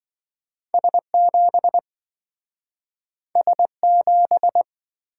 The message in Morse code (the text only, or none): S7 S7